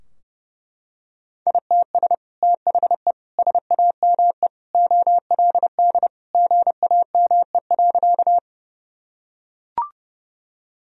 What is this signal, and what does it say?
Morse code, heard as a keyed tone, about 30 wpm, 700 Hz. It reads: ITS THE SAME OLD GAME.
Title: Sentences from Top 1000 Words - Review Part 3 - 30wpm